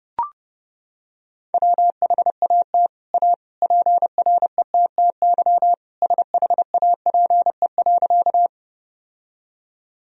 Morse code, heard as a keyed tone, about 30 wpm, 700 Hz. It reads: WHAT A PRETTY SHAPE.
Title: Sentences from Top 1000 Words - Review Part 3 - 30wpm